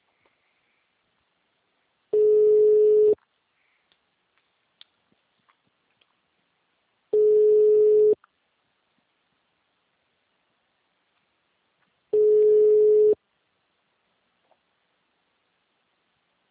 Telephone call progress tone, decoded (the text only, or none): ringback tone